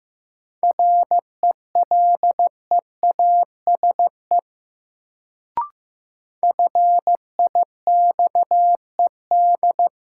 Morse code, heard as a keyed tone, about 15 wpm, 700 Hz. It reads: RELEASE FIXED
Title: Sentences from Top 1100 Words - New Word Drill Alt - 15wpm